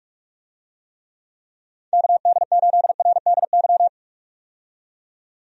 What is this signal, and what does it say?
Morse code, heard as a keyed tone, about 44 wpm, 700 Hz. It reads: KD8RDY
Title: USA Call Signs - 6 Characters - 45wpm